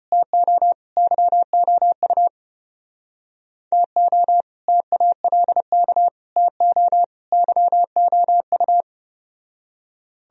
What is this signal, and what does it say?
Morse code, heard as a keyed tone, about 30 wpm, 700 Hz. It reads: TO YOU TO TALK TO YOU
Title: Speed-Racing - Sets of 4 Words 25wpm